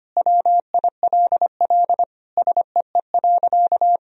Morse code, heard as a keyed tone, about 25 wpm, 700 Hz. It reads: WILL SEE.
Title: Sentences from Top 1000 Words - Review Part 1 - 25wpm